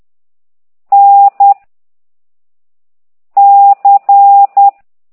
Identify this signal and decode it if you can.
Morse code, heard as a keyed tone, about 10 wpm, 800 Hz. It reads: N C
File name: ct2idw / exerc4